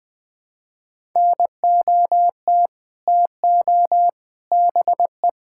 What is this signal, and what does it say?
Morse code, heard as a keyed tone, about 20 wpm, 700 Hz. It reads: NOT TO BE
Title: Sets of 6 Words - 20wpm